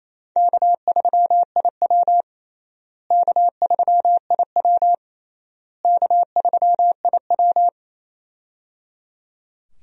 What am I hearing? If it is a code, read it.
Morse code, heard as a keyed tone, about 28 wpm, 700 Hz. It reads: K3IW K3IW K3IW
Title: USA Call Signs - 4 Characters - Repeated 3x - 28wpm